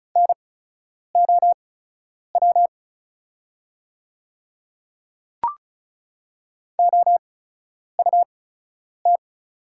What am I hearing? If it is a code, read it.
Morse code, heard as a keyed tone, about 36 wpm, spaced as for 10 wpm, 700 Hz. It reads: NOW OUT
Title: 8x Character Spacing - Top 100 Words - 35wpm